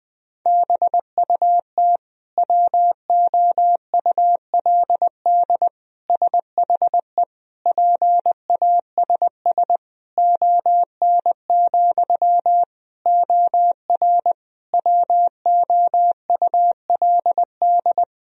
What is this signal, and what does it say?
Morse code, heard as a keyed tone, about 21 wpm, 700 Hz. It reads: BUT WOULD SHE PASS ON, OR WOULD
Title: Sentences from Top 1000 Words - Review Part 1 - 20wpm